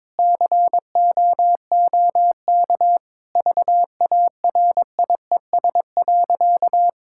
Morse code, heard as a keyed tone, about 22 wpm, 700 Hz. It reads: COOK VARIES.
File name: Sentences with 7-Word Phrases - 22wpm